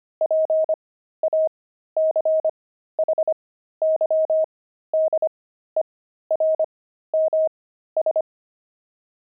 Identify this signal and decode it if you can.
Morse code, heard as a keyed tone, about 25 wpm, spaced as for 13 wpm, 630 Hz. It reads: PACHYDERMS
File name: pangram-fs-13-cs-25-jelly